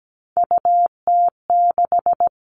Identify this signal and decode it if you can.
Morse code, heard as a keyed tone, about 17 wpm, 700 Hz. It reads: UT6
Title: All Cut Numbers - 3-Digit 17wpm